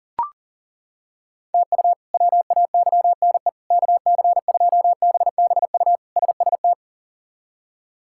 Morse code, heard as a keyed tone, about 37 wpm, 700 Hz. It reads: TU WAYNE KC2BBU SST
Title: K1USN SST QSO Practice - 40wpm